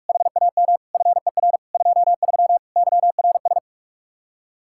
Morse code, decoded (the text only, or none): HAM FER 23 YRS